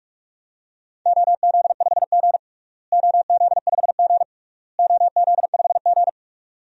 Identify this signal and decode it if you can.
Morse code, heard as a keyed tone, about 41 wpm, 690 Hz. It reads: OZ5G OZ5G OZ5G
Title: DXCC-100 Call Signs - 4 Characters - Repeated 3x - 45wpm